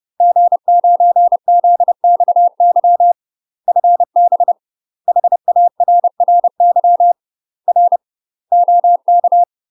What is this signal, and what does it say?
Morse code, heard as a keyed tone, about 30 wpm, 700 Hz. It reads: G9ZXY FB HARRY R OK
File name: UK MASTER QSOs TRAINING / 30WPM / UK MASTER QSOs 0015